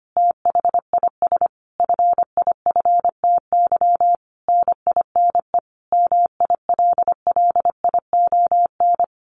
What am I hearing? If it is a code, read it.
Morse code, heard as a keyed tone, about 25 wpm, 700 Hz. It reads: THIS FIFTY NINE MILLION
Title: A Sound Of Thunder 25wpm